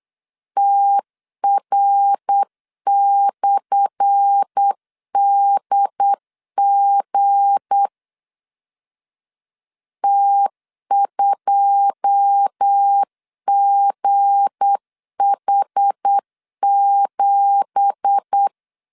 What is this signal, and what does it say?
Morse code, heard as a keyed tone, about 8 wpm, 800 Hz. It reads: TR/DG T2GH7
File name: LEZIONI / 10 SIGNS / 10B INT 8wpm 3534